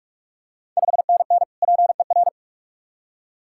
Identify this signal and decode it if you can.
Morse code, heard as a keyed tone, about 46 wpm, 690 Hz. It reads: 5NN PER
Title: Ontario QSO Party Exchange 45wpm